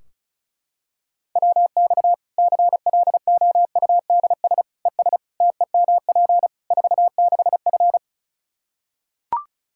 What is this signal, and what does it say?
Morse code, heard as a keyed tone, about 36 wpm, 700 Hz. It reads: WX CLOUDS ES TEMP 46F
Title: Practice QSO - Exchange 2 - Info - 35wpm